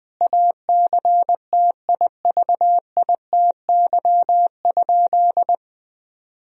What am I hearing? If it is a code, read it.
Morse code, heard as a keyed tone, about 20 wpm, 700 Hz. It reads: ACTIVITY?
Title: Sentences from Top 1000 Words - Review Part 4 - 20wpm